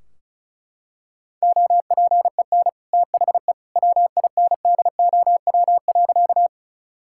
Morse code, heard as a keyed tone, about 36 wpm, 700 Hz. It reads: OPEN THE WINDOW.
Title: Sentences from Top 1600 Words 35wpm